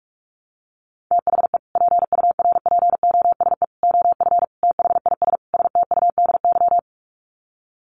Morse code, heard as a keyed tone, about 44 wpm, 700 Hz. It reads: THE PURPOSE OF THIS STUDY
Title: Sets of 5 Words 45wpm